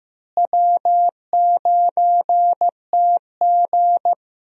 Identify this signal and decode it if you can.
Morse code, heard as a keyed tone, about 15 wpm, 700 Hz. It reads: W9TG
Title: USA Call Signs - 4 Characters - Repeated 2x - 15wpm